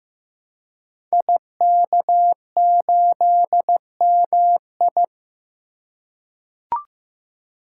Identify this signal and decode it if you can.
Morse code, heard as a keyed tone, about 15 wpm, 700 Hz. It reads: IK8MI
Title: Code Groups - 5 Letter:Numbers - 15wpm